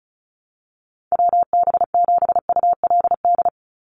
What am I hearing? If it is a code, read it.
Morse code, heard as a keyed tone, about 34 wpm, 700 Hz. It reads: WB7ULD
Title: NA SOTA Chaser Callsigns 35wpm